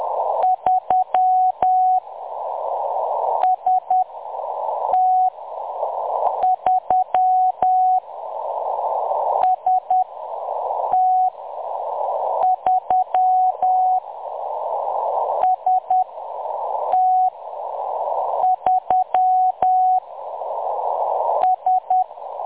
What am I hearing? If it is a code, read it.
Morse code, heard as a keyed tone, about 10 wpm, spaced as for 6 wpm, 740 Hz. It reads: 3 ST3 ST3 ST3 S
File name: M23-ST3- -0858utc-10184khz